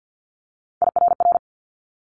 Morse code, heard as a keyed tone, about 50 wpm, 700 Hz. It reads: IRR
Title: Multi-letter Phonemes Quick Fire 50wpm